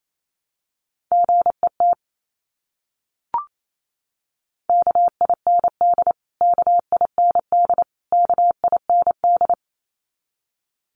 Morse code, heard as a keyed tone, about 28 wpm, 700 Hz. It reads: GET KIND KIND KIND
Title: Top 500 Words Repeated 28wpm